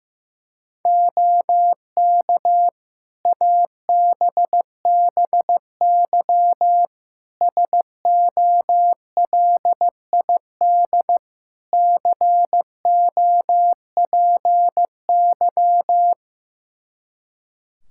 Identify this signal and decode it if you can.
Morse code, heard as a keyed tone, about 15 wpm, 700 Hz. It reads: OK ABBY SOLID COPY
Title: Practice QSO - Exchange 3 - More Info - 15wpm